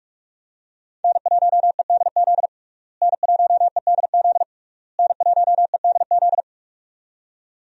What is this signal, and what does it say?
Morse code, heard as a keyed tone, about 41 wpm, 700 Hz. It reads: N1EDZ N1EDZ N1EDZ
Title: USA Call Signs - 5 Characters - Repeated 3x - 45wpm